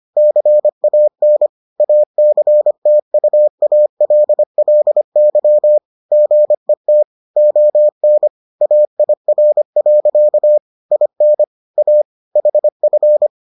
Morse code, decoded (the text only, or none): CAN ACTUALLY GET ON AIR. IN A HF